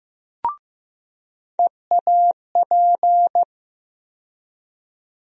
Morse code, heard as a keyed tone, about 15 wpm, 700 Hz. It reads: EAP
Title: USA Call Sign Postfixes - 3 Letters 15wpm